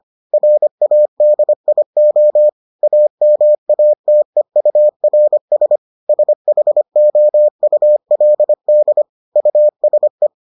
Morse code, heard as a keyed tone, about 25 wpm, 600 Hz. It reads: RADIO AMATEURS SHOULD USE